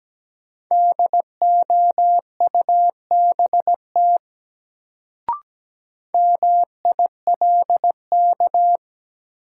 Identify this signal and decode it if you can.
Morse code, heard as a keyed tone, about 17 wpm, 700 Hz. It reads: DOUBT MILK
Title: Sentences from Top 1500 Words - New Word Drill Alt - 17wpm